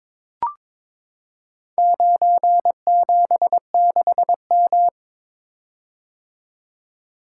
Morse code, heard as a keyed tone, about 23 wpm, 700 Hz. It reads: 976M
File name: practice / 22wpm / Four Letter-Number Combinations - 22wpm